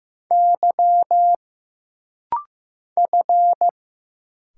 Morse code, heard as a keyed tone, about 15 wpm, 700 Hz. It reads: Y F
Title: Single Letters - Rapid-Fire 15wpm